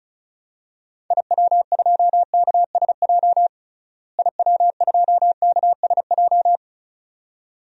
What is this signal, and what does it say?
Morse code, heard as a keyed tone, about 36 wpm, 700 Hz. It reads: IW2KSJ IW2KSJ
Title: DXCC-100 Call Signs - 6 Characters - Repeated 2x - 35wpm